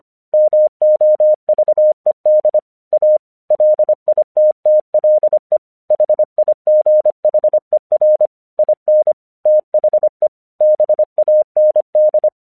Morse code, heard as a keyed tone, about 24 wpm, 610 Hz. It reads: MOVED A LITTLE HIGHER IN THE BAND
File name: 1 English 25wpm / AQC29